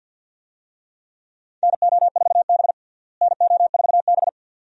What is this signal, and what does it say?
Morse code, heard as a keyed tone, about 52 wpm, 700 Hz. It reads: NO4B NO4B
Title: USA Call Signs - 4 Characters - Repeated 2x - 50wpm